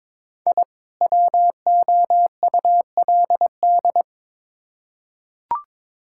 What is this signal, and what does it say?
Morse code, heard as a keyed tone, about 22 wpm, 700 Hz. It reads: I WOULD
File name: Build A Phrase - 2-Word Phrases - 22wpm